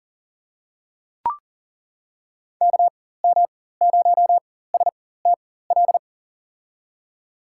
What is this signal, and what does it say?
Morse code, heard as a keyed tone, about 40 wpm, spaced as for 19 wpm, 700 Hz. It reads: KM0STL
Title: USA Call Signs - 6 Characters - 4x Character Spacing - 40wpm